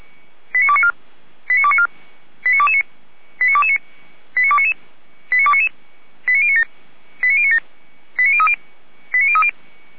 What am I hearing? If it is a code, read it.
Five-tone selective call (ZVEI), heard as quick runs of five tones, tones of about 70 ms. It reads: 89294 89294 89229 89229 89299 89299 89097 89097 89030 89030